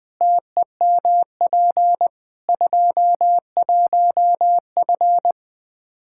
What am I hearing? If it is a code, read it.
Morse code, heard as a keyed tone, about 21 wpm, 700 Hz. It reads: TEMP 21F
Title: Practice QSO - Exchange 2 - Info - 20wpm